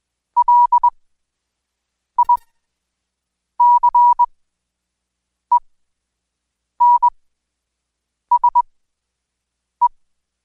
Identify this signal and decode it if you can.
Morse code, heard as a keyed tone, about 21 wpm, spaced as for 6 wpm, 970 Hz. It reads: LICENSE
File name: Code Quick/AudioDownld2/Seg109-Day14-DayDrill 14